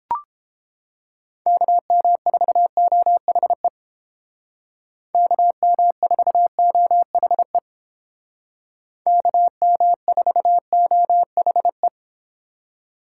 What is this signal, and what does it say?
Morse code, heard as a keyed tone, about 29 wpm, 700 Hz. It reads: KM4OHE KM4OHE KM4OHE